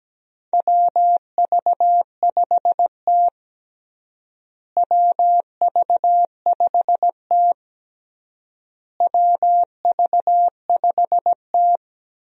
Morse code, heard as a keyed tone, about 17 wpm, 700 Hz. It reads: WV5T WV5T WV5T
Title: USA Call Signs - 4 Characters - Repeated 3x - 17wpm